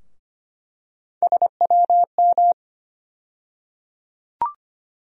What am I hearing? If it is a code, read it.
Morse code, heard as a keyed tone, about 25 wpm, 700 Hz. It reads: SWM
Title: USA Call Sign Postfixes - 3 Letters 25wpm